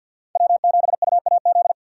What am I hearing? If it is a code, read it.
Morse code, heard as a keyed tone, about 63 wpm, 700 Hz. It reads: W7UAZ